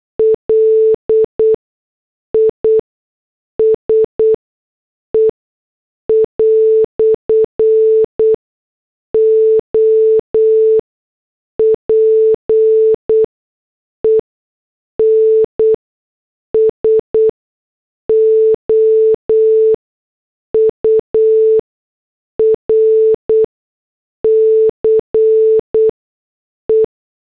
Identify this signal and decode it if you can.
Morse code, heard as a keyed tone, about 8 wpm, spaced as for 6 wpm, 440 Hz. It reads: LISE "OPEN SOURCE